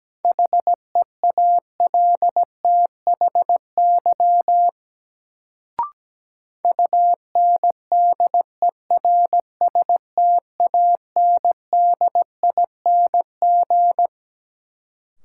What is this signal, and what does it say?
Morse code, heard as a keyed tone, about 17 wpm, 700 Hz. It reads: HEALTHY UNDERSTANDING
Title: Sentences from Top 1400 Words - New Word Drill Alt - 17wpm